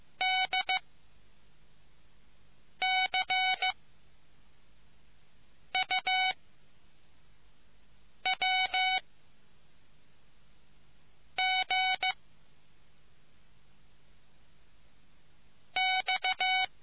Morse code, heard as a keyed tone, about 15 wpm, 2250 Hz. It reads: D C U W G X